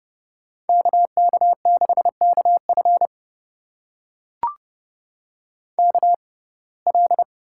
Morse code, heard as a keyed tone, about 30 wpm, spaced as for 22 wpm, 700 Hz. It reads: KK6KF KL
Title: USA Call Signs - 5 Characters - 6x Character Spacing - 30wpm